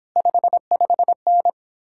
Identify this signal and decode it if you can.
Morse code, heard as a keyed tone, about 29 wpm, 700 Hz. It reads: 55N